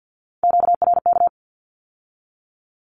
Morse code, heard as a keyed tone, about 50 wpm, 700 Hz. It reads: QRK
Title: Q-Codes with Spoken Meaning - 50wpm